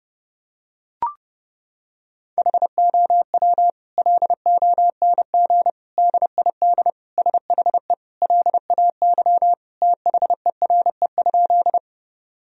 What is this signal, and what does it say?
Morse code, heard as a keyed tone, about 30 wpm, 700 Hz. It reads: HOW LONG DID SHE LAY THERE?
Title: Sentences from Top 1000 Words - Review Part 2 - 30wpm